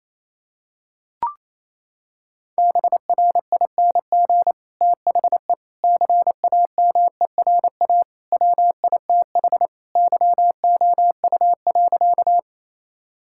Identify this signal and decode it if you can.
Morse code, heard as a keyed tone, about 28 wpm, 700 Hz. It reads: BRING THE CAMERA WITH YOU.
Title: Sentences from Top 1000 Words - Review Part 5 - 28wpm